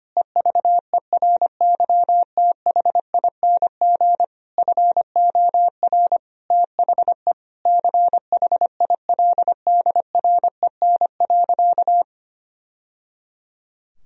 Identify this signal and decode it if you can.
Morse code, heard as a keyed tone, about 25 wpm, 700 Hz. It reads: EVERYTHING FOR THE CHILDREN.
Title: Sentences from Top 1000 Words - Review Part 2 - 25wpm